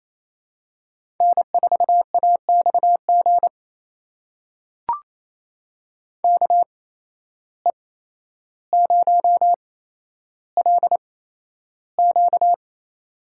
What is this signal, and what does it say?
Morse code, heard as a keyed tone, about 28 wpm, spaced as for 12 wpm, 700 Hz. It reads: N4AXG KE0LQ
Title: USA Call Signs - 5 Characters - 8x Character Spacing - 28wpm